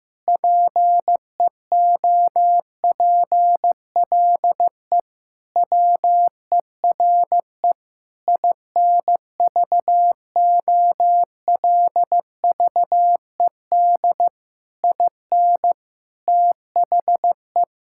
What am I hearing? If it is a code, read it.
Morse code, heard as a keyed tone, about 15 wpm, 700 Hz. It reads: PEOPLE WERE INVOLVED IN THE
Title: Sentences from Top 1000 Words - Review Part 5 - 15wpm